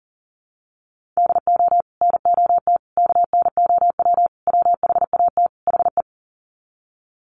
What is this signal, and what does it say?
Morse code, heard as a keyed tone, about 40 wpm, 700 Hz. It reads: DO NOT KNOW WHAT HE
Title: Sets of 5 Words Encore2 40wpm